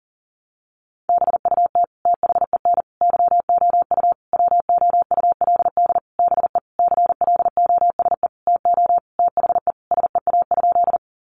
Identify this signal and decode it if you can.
Morse code, heard as a keyed tone, about 37 wpm, 700 Hz. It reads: BUT THEN YOU WOULD BE CLOSE TO THE SEA?
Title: Sentences from Top 300 Words 40wpm